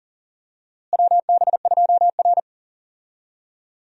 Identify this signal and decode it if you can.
Morse code, heard as a keyed tone, about 40 wpm, 700 Hz. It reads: WB2R
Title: USA Call Signs - 4 Characters - Repeated 2x - 40wpm